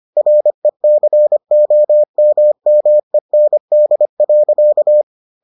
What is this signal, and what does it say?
Morse code, heard as a keyed tone, about 25 wpm, 600 Hz. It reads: RECOMMEND.